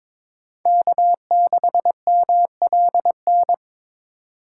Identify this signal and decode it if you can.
Morse code, heard as a keyed tone, about 21 wpm, 700 Hz. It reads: K6MLN
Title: USA Call Signs - 5 Characters - 8x Character Spacing - 22wpm